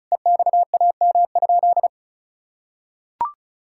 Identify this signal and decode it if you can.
Morse code, heard as a keyed tone, about 36 wpm, 700 Hz. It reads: EXAM?
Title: Sentences from Top 1000 Words - Review Part 5 - 35wpm